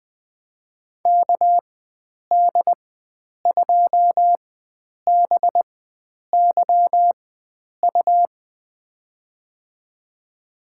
Morse code, heard as a keyed tone, about 20 wpm, spaced as for 9 wpm, 700 Hz. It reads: KD2BYU
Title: USA Call Signs - 4x Character Spacing - 20wpm